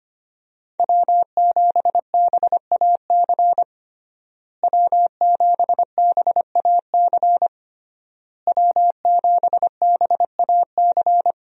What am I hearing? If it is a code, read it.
Morse code, heard as a keyed tone, about 25 wpm, 700 Hz. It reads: W7BAC W7BAC W7BAC
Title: USA Call Signs - 5 Characters - Repeated 3x - 25wpm